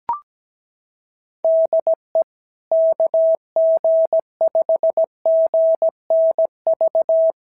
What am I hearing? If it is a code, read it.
Morse code, heard as a keyed tone, about 17 wpm, 650 Hz. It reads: DE KG5GNV
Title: SOTA QSO Practice 17wpm